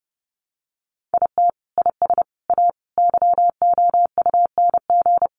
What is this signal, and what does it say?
Morse code, heard as a keyed tone, about 28 wpm, 700 Hz. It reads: IT IS A YOUNG
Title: Sentences from Top 400 Words 30wpm